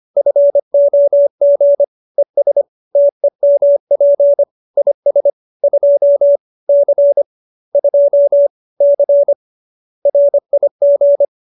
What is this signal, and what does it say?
Morse code, heard as a keyed tone, about 25 wpm, 570 Hz. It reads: FOG ES TEMP IS 2 C 2 C RIG